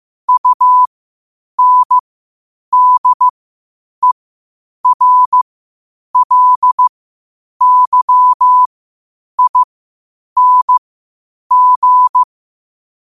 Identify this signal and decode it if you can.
Morse code, heard as a keyed tone, about 15 wpm, spaced as for 8 wpm, 1000 Hz. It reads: UNDERLYING
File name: qotdfiles / 8wpm